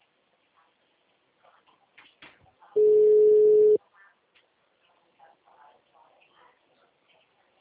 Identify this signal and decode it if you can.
Ringback tone (Europe, 425 Hz).
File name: out- -2011- - - .47200